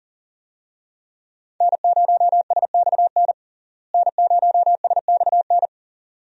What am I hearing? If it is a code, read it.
Morse code, heard as a keyed tone, about 40 wpm, 700 Hz. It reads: N0SXN N0SXN
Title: USA Call Signs - 5 Characters - Repeated 2x - 40wpm